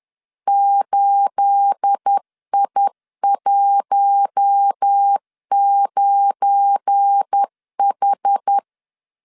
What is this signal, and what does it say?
Morse code, heard as a keyed tone, about 10 wpm, 800 Hz. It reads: 8I19H